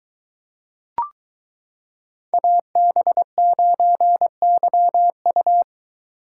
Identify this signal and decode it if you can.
Morse code, heard as a keyed tone, about 22 wpm, 700 Hz. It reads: AB9YU